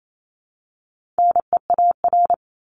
Morse code, heard as a keyed tone, about 28 wpm, 700 Hz. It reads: NEAR
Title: Top 200 Words Repeated 28wpm